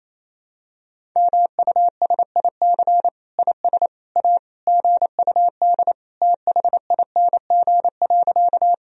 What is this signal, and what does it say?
Morse code, heard as a keyed tone, about 28 wpm, 700 Hz. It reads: MUSIC IS A GUD THING.
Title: Sentences from Top 300 Words and Abbreviations 28wpm